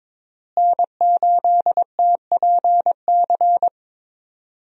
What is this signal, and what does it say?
Morse code, heard as a keyed tone, about 22 wpm, 700 Hz. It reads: N8TPC
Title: USA Call Signs - 5 Characters - Repeated 2x - 22wpm